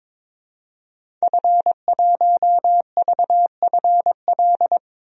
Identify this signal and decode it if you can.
Morse code, heard as a keyed tone, about 21 wpm, 700 Hz. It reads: F1VFL
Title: DXCC-100 Call Signs - 5 Characters - Repeated 3x - 22wpm